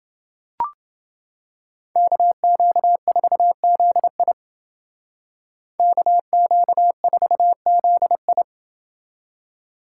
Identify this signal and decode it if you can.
Morse code, heard as a keyed tone, about 28 wpm, 700 Hz. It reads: KQ4ZI KQ4ZI